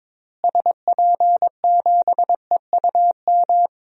Morse code, heard as a keyed tone, about 22 wpm, 700 Hz. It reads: SP7EUM
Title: DXCC-100 Call Signs - 6 Characters - Repeated 2x - 22wpm